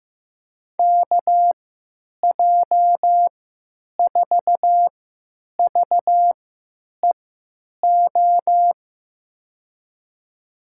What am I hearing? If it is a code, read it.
Morse code, heard as a keyed tone, about 15 wpm, spaced as for 9 wpm, 700 Hz. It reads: KJ4VEO
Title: USA Call Signs - 3x Character Spacing - 15wpm